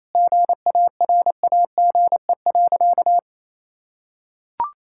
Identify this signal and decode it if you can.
Morse code, heard as a keyed tone, about 28 wpm, 700 Hz. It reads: GARAGE.
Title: Sentences from Top 1600 Words 28wpm